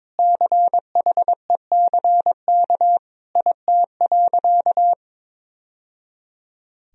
Morse code, heard as a keyed tone, about 22 wpm, 700 Hz. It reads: CHECK IT.
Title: Sentences from Top 1000 Words - Review Part 3 - 22wpm